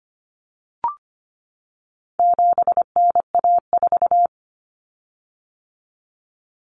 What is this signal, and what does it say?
Morse code, heard as a keyed tone, about 25 wpm, 700 Hz. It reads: 7NA4
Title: Common Cut Numbers - 4-Digit 25wpm